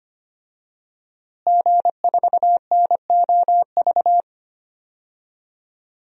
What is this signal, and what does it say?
Morse code, heard as a keyed tone, about 25 wpm, 700 Hz. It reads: G4NOV
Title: DXCC-100 Call Signs - 5 Characters - Repeated 2x - 25wpm